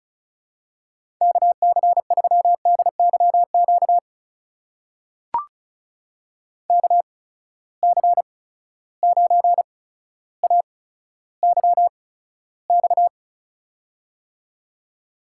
Morse code, heard as a keyed tone, about 36 wpm, spaced as for 15 wpm, 700 Hz. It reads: KC3DYQ KC9AYX
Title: USA Call Signs - 6 Characters - 8x Character Spacing - 35wpm